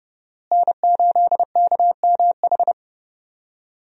Morse code, heard as a keyed tone, about 30 wpm, 700 Hz. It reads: N8KMH
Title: USA Call Signs - 5 Characters - Repeated 3x - 30wpm